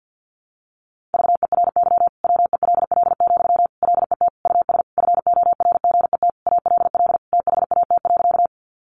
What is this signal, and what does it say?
Morse code, heard as a keyed tone, about 48 wpm, 700 Hz. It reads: VERY WELL, LET US FORGET ALL THAT.
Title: Sentences from Top 1200 Words 50wpm